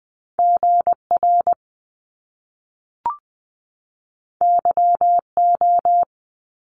Morse code, heard as a keyed tone, about 20 wpm, 700 Hz. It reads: G R Y O
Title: Top Two Letter Combinations in Words 20wpm